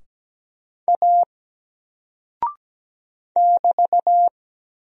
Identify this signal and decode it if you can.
Morse code, heard as a keyed tone, about 17 wpm, 700 Hz. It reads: A <BT>
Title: Single Character (Letter:Number:Punctuation:Prosign) Alt 17wpm